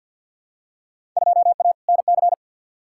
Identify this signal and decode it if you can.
Morse code, heard as a keyed tone, about 52 wpm, 700 Hz. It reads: 2A NC
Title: Field Day Exchanges - 50wpm